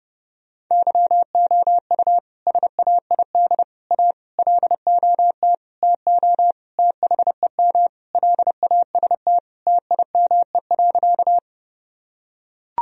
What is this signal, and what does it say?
Morse code, heard as a keyed tone, about 28 wpm, 700 Hz. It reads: YOU SAID A LOT TO THEM LAST TIME.
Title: Sentences from Top 1000 Words - Review Part 3 - 30wpm